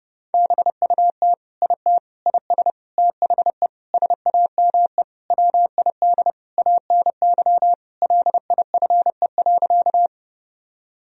Morse code, heard as a keyed tone, about 31 wpm, 700 Hz. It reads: BUT IT IS THE SAME WID ANY LIFE.
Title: Sentences from Top 300 Words and Abbreviations 30wpm